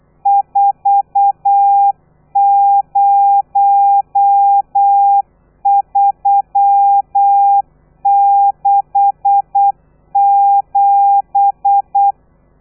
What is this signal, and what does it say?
Morse code, heard as a keyed tone, about 8 wpm, 800 Hz. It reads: 40367
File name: telegrafia / i2tfj / Lez13 35